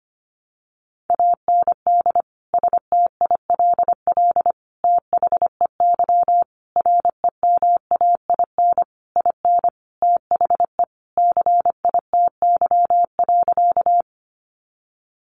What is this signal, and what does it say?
Morse code, heard as a keyed tone, about 25 wpm, 700 Hz. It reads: AND STILL THEY REMAIN IN THE CITY.
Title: Sentences from Top 1200 Words 25wpm